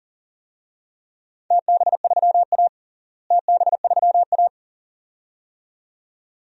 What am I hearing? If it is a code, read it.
Morse code, heard as a keyed tone, about 40 wpm, 700 Hz. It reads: TB3A TB3A
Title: DXCC-100 Call Signs - 4 Characters - Repeated 2x - 40wpm